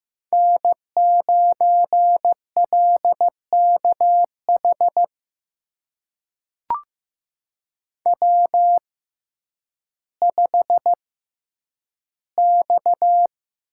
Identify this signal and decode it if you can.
Morse code, heard as a keyed tone, about 16 wpm, spaced as for 9 wpm, 700 Hz. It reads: N9LKH W5X
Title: USA Call Signs - 5 Characters - 6x Character Spacing - 15wpm